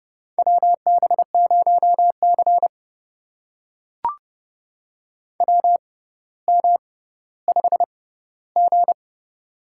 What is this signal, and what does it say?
Morse code, heard as a keyed tone, about 30 wpm, spaced as for 15 wpm, 690 Hz. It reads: WB0C WM5G
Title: USA Call Signs - 4 Characters - 6x Character Spacing - 30wpm